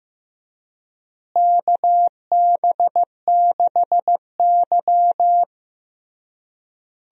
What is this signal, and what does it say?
Morse code, heard as a keyed tone, about 15 wpm, 700 Hz. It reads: KB6Y
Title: USA Call Signs - 4 Characters - Repeated 3x - 15wpm